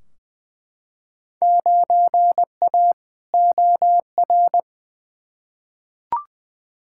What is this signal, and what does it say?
Morse code, heard as a keyed tone, about 20 wpm, 700 Hz. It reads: 9A OR
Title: Field Day Exchanges Encore - 20wpm